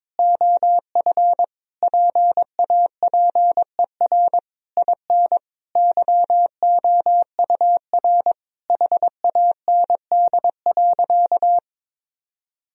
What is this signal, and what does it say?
Morse code, heard as a keyed tone, about 22 wpm, 700 Hz. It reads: OF PAPER IN YOUR HAND.
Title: Sentences from Top 1000 Words - Review Part 2 - 22wpm